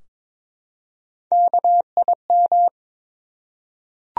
Morse code, heard as a keyed tone, about 22 wpm, 700 Hz. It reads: KIM
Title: USA Common Names - Max 3 Letters - 22wpm